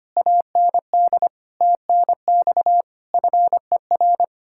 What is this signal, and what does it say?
Morse code, heard as a keyed tone, about 32 wpm, 700 Hz. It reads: AND TNX FER